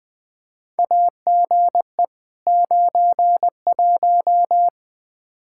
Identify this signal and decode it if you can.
Morse code, heard as a keyed tone, about 20 wpm, 700 Hz. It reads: AGE 91
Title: Practice QSO - Exchange 3 - More Info - 20wpm